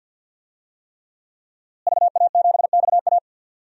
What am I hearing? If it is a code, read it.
Morse code, heard as a keyed tone, about 51 wpm, 700 Hz. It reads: VA7XA
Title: DXCC-100 Call Signs - 5 Characters - Repeated 3x - 50wpm